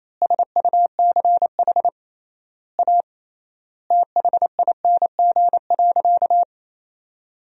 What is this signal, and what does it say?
Morse code, heard as a keyed tone, about 28 wpm, 700 Hz. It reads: SUCH A THING.
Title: 3x Word-Spacing - Sentences from Top 200 Words - 28wpm